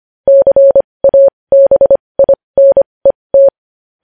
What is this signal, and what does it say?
Morse code, heard as a keyed tone, about 25 wpm, spaced as for 20 wpm, 550 Hz. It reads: CABINET
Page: news-headlines-in-morse-code-at-20-wpm